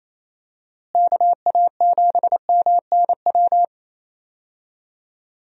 Morse code, heard as a keyed tone, about 28 wpm, 700 Hz. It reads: KA7MNW
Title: USA Call Signs - 6 Characters - Repeated 3x - 28wpm